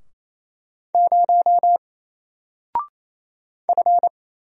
Morse code, heard as a keyed tone, about 27 wpm, 700 Hz. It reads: 0 F
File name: Single-Letter-Number-28wpm